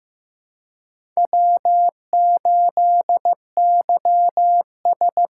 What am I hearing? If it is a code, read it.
Morse code, heard as a keyed tone, about 15 wpm, 700 Hz. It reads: W8YS
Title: USA Call Signs - 4x Character Spacing - 15wpm